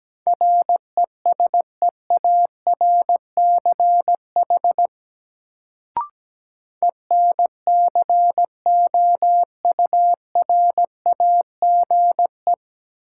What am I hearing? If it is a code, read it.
Morse code, heard as a keyed tone, about 17 wpm, 700 Hz. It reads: RESEARCH ENCOURAGE
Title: Sentences from Top 1400 Words - New Word Drill Alt - 17wpm